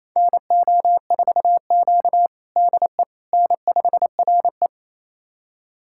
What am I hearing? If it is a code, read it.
Morse code, heard as a keyed tone, about 28 wpm, 700 Hz. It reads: NO4Q DE N5RE
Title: USA Callsign DE Callsign - 4 Characters - 28wpm